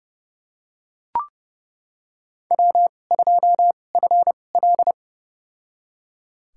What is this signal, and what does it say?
Morse code, heard as a keyed tone, about 30 wpm, spaced as for 22 wpm, 700 Hz. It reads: W2FL
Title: USA Call Signs - 4 Characters - 2x Character Spacing - 30wpm